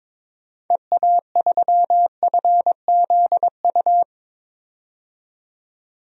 Morse code, heard as a keyed tone, about 21 wpm, 700 Hz. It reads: EA3FZU
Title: DXCC-100 Call Signs - 6 Characters - 22wpm